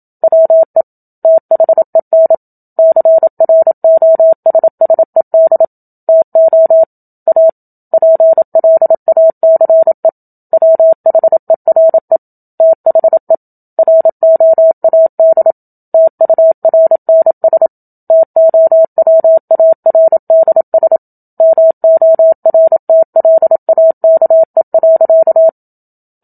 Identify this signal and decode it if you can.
Morse code, heard as a keyed tone, about 27 wpm, 650 Hz. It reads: WE THEN CROSSED TO A PLACE WHERE THE ROAD TURNS TOWARDS MORTLAKE.
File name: Book Two / 01-Chapter 1 - 27 WPM